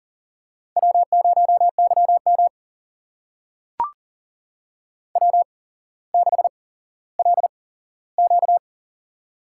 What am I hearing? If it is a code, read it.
Morse code, heard as a keyed tone, about 40 wpm, spaced as for 17 wpm, 700 Hz. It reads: W0YM W6LQ
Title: USA Call Signs - 4 Characters - 8x Character Spacing - 40wpm